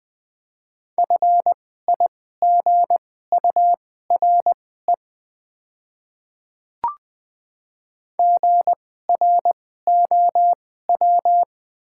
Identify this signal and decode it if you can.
Morse code, heard as a keyed tone, about 20 wpm, spaced as for 14 wpm, 700 Hz. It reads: FIGURE GROW
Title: 2x Character Spacing - Top 500 Words - 20wpm